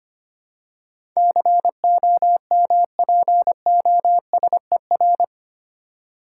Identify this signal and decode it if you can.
Morse code, heard as a keyed tone, about 25 wpm, 700 Hz. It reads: COMPOSER
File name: Eight Letter Words - 25wpm